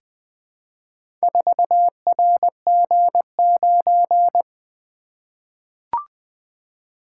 Morse code, heard as a keyed tone, about 20 wpm, 700 Hz. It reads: 4RG9
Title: Four Letter-Number Combinations - 20wpm